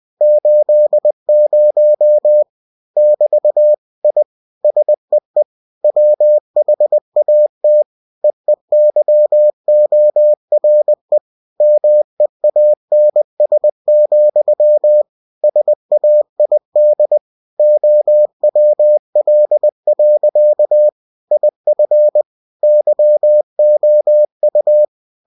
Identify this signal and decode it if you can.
Morse code, heard as a keyed tone, about 20 wpm, 600 Hz. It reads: 80 <BT> I SEE WHAT EEYORE MEANS, SAID OWL. IF YOU
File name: cw-books / winnie-the-pooh / winnie-20wpm-page-0080